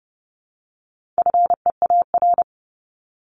Morse code, heard as a keyed tone, about 30 wpm, 700 Hz. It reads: FEAR
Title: Top 1751-2000 Words 30wpm